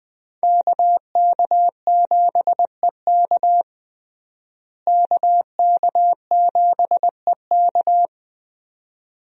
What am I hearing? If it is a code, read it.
Morse code, heard as a keyed tone, about 20 wpm, 700 Hz. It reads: KK7EK KK7EK
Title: USA Call Signs - 5 Characters - Repeated 2x - 20wpm